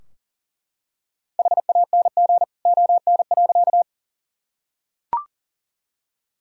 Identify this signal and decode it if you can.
Morse code, heard as a keyed tone, about 40 wpm, 700 Hz. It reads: HANG ON.
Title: Sentences from Top 1600 Words 40wpm